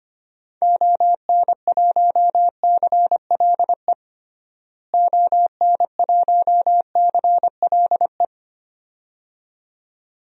Morse code, heard as a keyed tone, about 25 wpm, 700 Hz. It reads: ON1CLE ON1CLE
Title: DXCC-100 Call Signs - 6 Characters - Repeated 2x - 25wpm